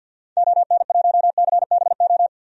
Morse code, heard as a keyed tone, about 48 wpm, 700 Hz. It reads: ON1CBO
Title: DXCC-100 Call Signs - 6 Characters - 50wpm